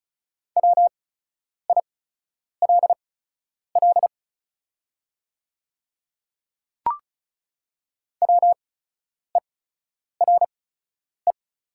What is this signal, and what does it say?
Morse code, heard as a keyed tone, about 36 wpm, spaced as for 10 wpm, 700 Hz. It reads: WILL WERE
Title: 8x Character Spacing - Top 100 Words - 35wpm